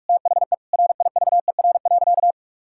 Morse code, heard as a keyed tone, about 41 wpm, 700 Hz. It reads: THE RIVER.
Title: Sentences from Top 1000 Words - Review Part 5 - 45wpm